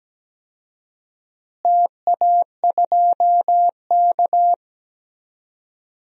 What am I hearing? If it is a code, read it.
Morse code, heard as a keyed tone, about 17 wpm, 700 Hz. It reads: TA2K
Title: DXCC-100 Call Signs - 4 Characters - Repeated 2x - 17wpm